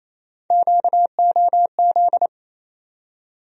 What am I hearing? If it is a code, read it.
Morse code, heard as a keyed tone, about 28 wpm, 700 Hz. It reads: QOZ
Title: USA Call Sign Postfixes - 3 Letters 28wpm